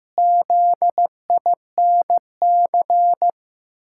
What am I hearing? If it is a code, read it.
Morse code, heard as a keyed tone, about 15 wpm, 700 Hz. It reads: ZINC
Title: Pangram Challenge v2 - 15wpm